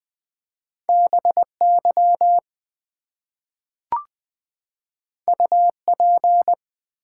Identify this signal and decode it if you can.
Morse code, heard as a keyed tone, about 20 wpm, 700 Hz. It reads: BY UP
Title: Two Letter Words - 20wpm